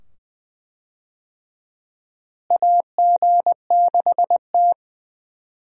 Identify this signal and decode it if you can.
Morse code, heard as a keyed tone, about 20 wpm, 700 Hz. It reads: AG6T
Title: USA Call Signs - 4 Characters - Repeated 2x - 20wpm